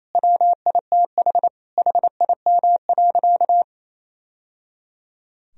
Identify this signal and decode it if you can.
Morse code, heard as a keyed tone, about 28 wpm, 700 Hz. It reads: WITH HIM.
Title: Sentences from Top 1000 Words - Review Part 2 - 28wpm